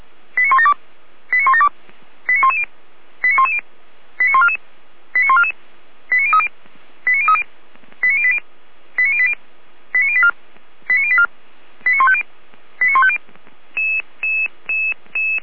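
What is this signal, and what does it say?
Five-tone selective call (ZVEI), heard as quick runs of five tones, tones of about 70 ms. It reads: 89172 89172 89110 89110 89144 89144 89030 89030 89080 89080 89084 89084 89150 89150
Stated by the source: ZVEI five-tone selective call received by SDR.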